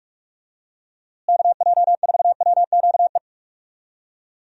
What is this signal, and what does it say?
Morse code, heard as a keyed tone, about 46 wpm, 700 Hz. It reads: KJ4WQE
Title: USA Call Signs - 6 Characters - Repeated 2x - 45wpm